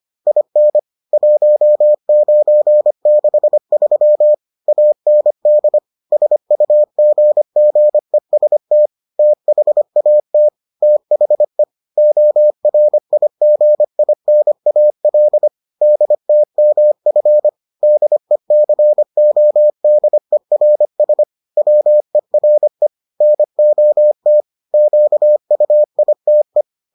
Morse code, heard as a keyed tone, about 25 wpm, 600 Hz. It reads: IN 1963 AND SUGGEST THAT THE ORIGINAL DTMF DECODERS WERE NOT QUITE